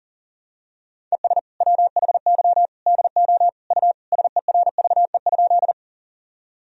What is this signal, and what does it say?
Morse code, heard as a keyed tone, about 40 wpm, 700 Hz. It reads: ES WHY DO U SERVE?
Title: Sentences from Top 400 Words and Abbreviations 40wpm